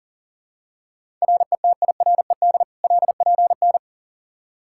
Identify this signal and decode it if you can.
Morse code, heard as a keyed tone, about 40 wpm, 700 Hz. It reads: RETIRED LPN
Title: Practice QSO - Exchange 3 - More Info - 40wpm